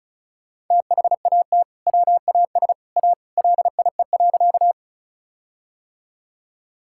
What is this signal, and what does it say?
Morse code, heard as a keyed tone, about 36 wpm, 700 Hz. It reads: THAT WAS A LIE.
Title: Sentences from Top 1600 Words 35wpm